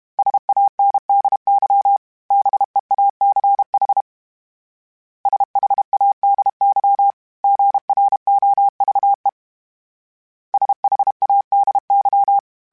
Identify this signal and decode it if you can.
Morse code, heard as a keyed tone, about 32 wpm, 800 Hz. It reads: SANDY BEACH SHADY GROVE SHADY
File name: fromSMs / FromSMs mp3s / Cities@30